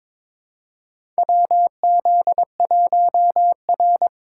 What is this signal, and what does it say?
Morse code, heard as a keyed tone, about 22 wpm, 700 Hz. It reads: WZ1R
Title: USA Call Signs - 4 Characters - 3x Character Spacing - 22wpm